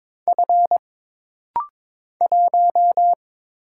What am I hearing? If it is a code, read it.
Morse code, heard as a keyed tone, about 22 wpm, 700 Hz. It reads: F 1
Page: This rapid-fire timing is geared to push your ICR (Instant Character Recognition) skills.